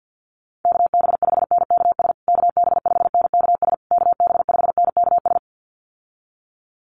Morse code, heard as a keyed tone, about 48 wpm, 700 Hz. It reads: KB5NKS KB5NKS KB5NKS
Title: USA Call Signs Repeated 3x 50wpm